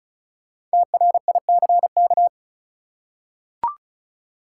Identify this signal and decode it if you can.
Morse code, heard as a keyed tone, about 36 wpm, 700 Hz. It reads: TRICK
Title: Five Letter Words - 35wpm